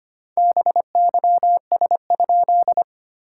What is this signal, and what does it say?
Morse code, heard as a keyed tone, about 25 wpm, 700 Hz. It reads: BYS?
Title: SOTA QSO Practice 25wpm